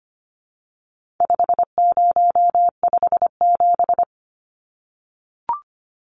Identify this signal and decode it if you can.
Morse code, heard as a keyed tone, about 25 wpm, 700 Hz. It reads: 5057
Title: Numbers - 4-Digit 25wpm